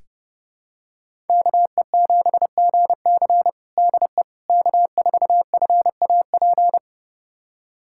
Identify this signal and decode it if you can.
Morse code, heard as a keyed tone, about 29 wpm, 700 Hz. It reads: KE7GC DE K4FAP
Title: USA Callsign DE Callsign 30wpm